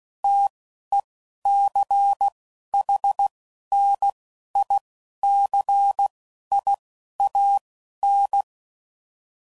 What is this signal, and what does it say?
Morse code, heard as a keyed tone, about 16 wpm, 800 Hz. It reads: T E C H N I C I A N